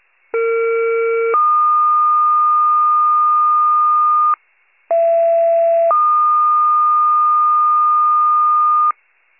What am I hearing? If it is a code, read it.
Two-tone page, heard as a one-second tone, then a three-second tone, about 460 then 1220 Hz, about 670 then 1220 Hz. It is heard from 0.3 to 8.9 s.
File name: pager / 21 RESCUE / wcalert